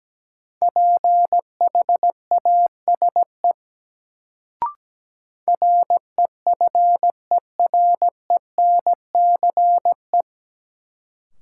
Morse code, heard as a keyed tone, about 17 wpm, 700 Hz. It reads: PHASE REFERENCE
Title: Sentences from Top 1200 Words - New Word Drill Alt - 17wpm